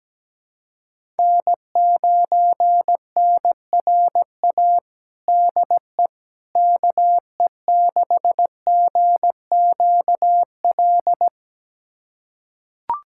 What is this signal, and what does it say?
Morse code, heard as a keyed tone, about 17 wpm, 700 Hz. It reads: N9NRA DE KE6GQL
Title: USA Callsign DE Callsign 17wpm